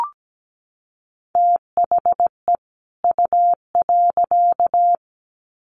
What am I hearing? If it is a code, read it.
Morse code, heard as a keyed tone, about 17 wpm, 700 Hz. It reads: THE U.
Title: Sets of 2 Words 17wpm